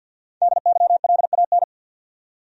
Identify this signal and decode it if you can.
Morse code, heard as a keyed tone, about 50 wpm, 700 Hz. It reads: DYLAN
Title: USA Common Names - Max 5 Letters - 50wpm